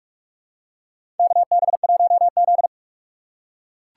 Morse code, heard as a keyed tone, about 44 wpm, 700 Hz. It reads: KB1Z
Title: USA Call Signs - 4 Characters - 4x Character Spacing - 45wpm